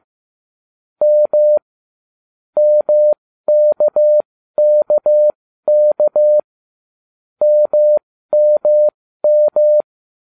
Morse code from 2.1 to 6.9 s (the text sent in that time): MKKK